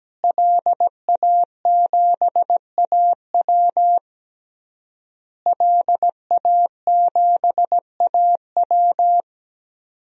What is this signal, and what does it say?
Morse code, heard as a keyed tone, about 17 wpm, 700 Hz. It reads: LA7AW LA7AW
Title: DXCC-100 Call Signs - 5 Characters - Repeated 3x - 17wpm